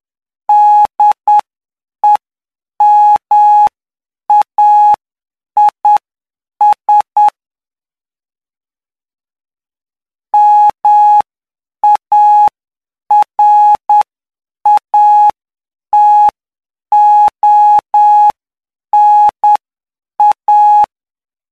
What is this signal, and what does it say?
Morse code, heard as a keyed tone, about 9 wpm, spaced as for 7 wpm, 840 Hz. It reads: DEMAIS MARATONA
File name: cw / py2lcd / 05-PY2LCD-Li��o 14